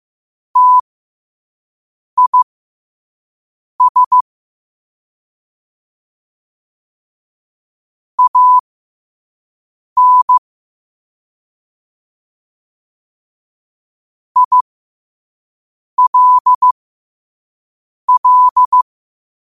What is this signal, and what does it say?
Morse code, heard as a keyed tone, about 15 wpm, spaced as for 5 wpm, 1000 Hz. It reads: TIS AN ILL
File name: 5wpm